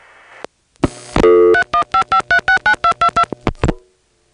DTMF, heard as telephone touch tones, tones of about 90 ms. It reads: 3155369222